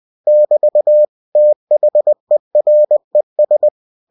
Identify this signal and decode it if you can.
Morse code, heard as a keyed tone, about 20 wpm, 600 Hz. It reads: <BT> THERES
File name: cw-books / winnie-the-pooh / winnie-20wpm-page-0093